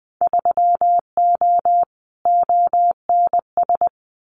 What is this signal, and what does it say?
Morse code, heard as a keyed tone, about 20 wpm, 700 Hz. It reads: 3O ONS
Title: Winter Field Day Exchanges Encore 20wpm